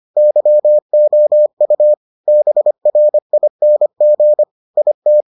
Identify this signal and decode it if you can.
Morse code, heard as a keyed tone, about 25 wpm, 600 Hz. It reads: YOU BRING IT